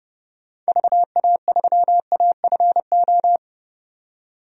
Morse code, heard as a keyed tone, about 32 wpm, 690 Hz. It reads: VA3AFO
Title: DXCC-100 Call Signs - 6 Characters - 30wpm